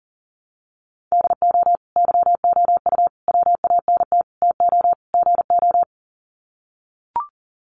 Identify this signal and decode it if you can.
Morse code, heard as a keyed tone, about 40 wpm, 700 Hz. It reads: DO YOU WANT TO GO
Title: Sets of 5 Words Encore2 40wpm